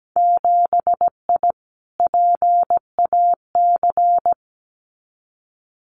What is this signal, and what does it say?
Morse code, heard as a keyed tone, about 17 wpm, 700 Hz. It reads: 7I PAC
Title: Winter Field Day Exchanges Encore 17wpm